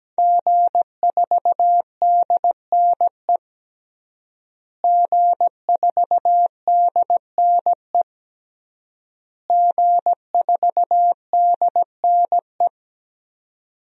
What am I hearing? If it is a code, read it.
Morse code, heard as a keyed tone, about 17 wpm, 700 Hz. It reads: G4DNE G4DNE G4DNE
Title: DXCC-100 Call Signs - 5 Characters - Repeated 3x - 17wpm